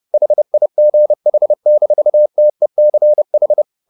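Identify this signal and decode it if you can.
Morse code, heard as a keyed tone, about 30 wpm, 600 Hz. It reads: HIGH-TECH
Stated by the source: Morse code at 29 wpm, sending Ken Follett's novel Never in English, from CW med Gnister.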